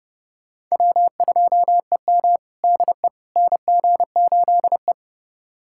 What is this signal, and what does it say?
Morse code, heard as a keyed tone, about 32 wpm, 700 Hz. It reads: W2EM DE NG8E
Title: USA Callsign DE Callsign - 4 Characters - 30wpm